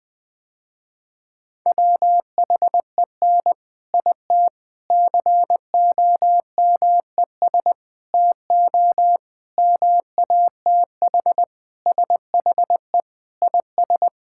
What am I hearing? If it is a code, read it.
Morse code, heard as a keyed tone, about 21 wpm, 700 Hz. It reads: WHEN IT COMES TO MATH SHE IS
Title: Sentences with 3-Word Phrases - Spoken Beginning - 20wpm